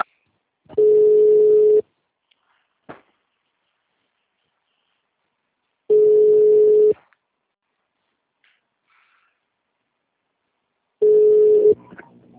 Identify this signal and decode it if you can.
Ringback tone (Europe, 425 Hz).